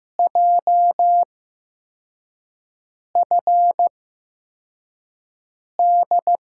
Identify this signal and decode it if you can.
Morse code, heard as a keyed tone, about 15 wpm, spaced as for 4 wpm, 700 Hz. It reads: JFD
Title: USA Call Signs - 8x Character Spacing - 15wpm